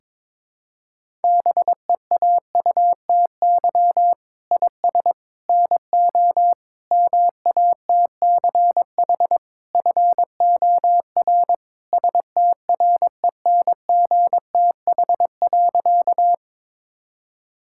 Morse code, heard as a keyed tone, about 22 wpm, 700 Hz. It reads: BEAUTY IS NO MATCH FOR STRENGTH.
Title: Sentences from Top 1600 Words 22wpm